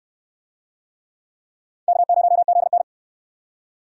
Morse code, heard as a keyed tone, about 62 wpm, 700 Hz. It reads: K0ZN